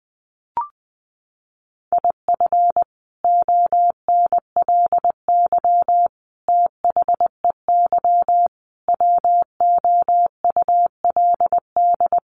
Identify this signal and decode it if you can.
Morse code, heard as a keyed tone, about 20 wpm, 700 Hz. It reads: IF ONLY THEY WOULD
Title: Sentences from Top 500 Words 20wpm